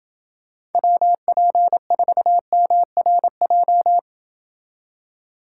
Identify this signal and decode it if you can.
Morse code, heard as a keyed tone, about 30 wpm, 700 Hz. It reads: WP4MRJ